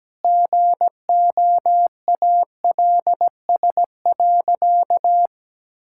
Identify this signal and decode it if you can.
Morse code, heard as a keyed tone, about 17 wpm, 700 Hz. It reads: GOALS.
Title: Sentences from Top 1000 Words - Review Part 6 - 17wpm